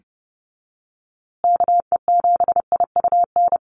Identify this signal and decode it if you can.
Morse code, heard as a keyed tone, about 30 wpm, 700 Hz. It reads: KE7IUN
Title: USA Call Signs Repeated 3x 30wpm